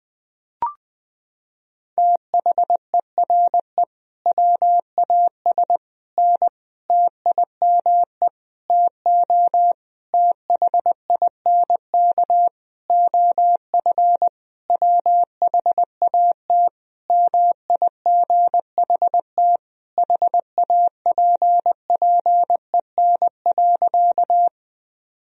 Morse code, heard as a keyed tone, about 20 wpm, 700 Hz. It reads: THERE WAS N TIME TO THINK OF WHAT MIGHT HAPPEN.
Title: Sentences from Top 400 Words and Abbreviations 20wpm